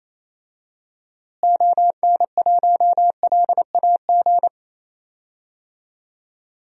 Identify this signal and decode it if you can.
Morse code, heard as a keyed tone, about 28 wpm, 700 Hz. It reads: ON1LAG
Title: DXCC-100 Call Signs - 6 Characters - Repeated 3x - 28wpm